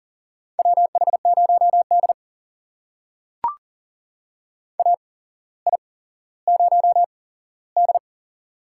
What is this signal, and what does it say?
Morse code, heard as a keyed tone, about 40 wpm, spaced as for 17 wpm, 700 Hz. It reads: WH0D AI0D
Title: USA Call Signs - 4 Characters - 8x Character Spacing - 40wpm